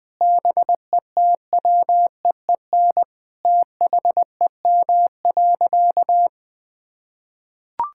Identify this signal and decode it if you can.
Morse code, heard as a keyed tone, about 21 wpm, 700 Hz. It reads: BETWEEN THEM.
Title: Sentences from Top 1000 Words - Review Part 5 - 20wpm